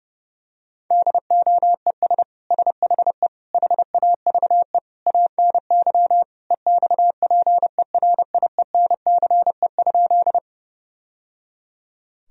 Morse code, heard as a keyed tone, about 30 wpm, 700 Hz. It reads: DOES SHE HAVE ANY EXPERIENCE?
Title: Sentences from Top 1000 Words - Review Part 3 - 30wpm